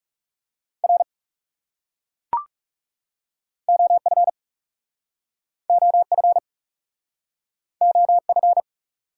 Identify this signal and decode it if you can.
Morse code, heard as a keyed tone, about 40 wpm, 700 Hz. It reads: R OF OF OF